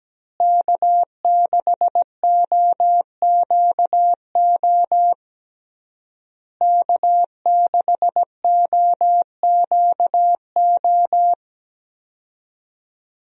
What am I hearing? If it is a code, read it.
Morse code, heard as a keyed tone, about 17 wpm, 700 Hz. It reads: K6OQO K6OQO
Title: USA Call Signs - 5 Characters - Repeated 3x - 17wpm